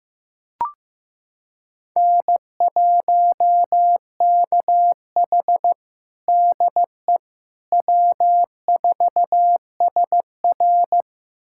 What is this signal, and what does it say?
Morse code, heard as a keyed tone, about 16 wpm, 700 Hz. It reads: N1KH DE W4SR
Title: USA Callsign DE Callsign - 4 Characters - 15wpm